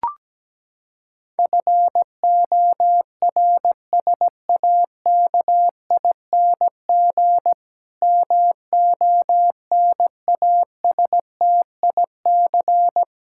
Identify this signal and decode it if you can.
Morse code, heard as a keyed tone, about 17 wpm, 700 Hz. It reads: FORSAKING MONASTIC
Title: Pangram Challenge v2 - 17wpm